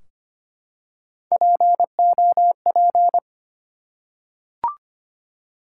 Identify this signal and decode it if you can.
Morse code, heard as a keyed tone, about 25 wpm, 700 Hz. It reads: POP
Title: Three Letter Words - 25wpm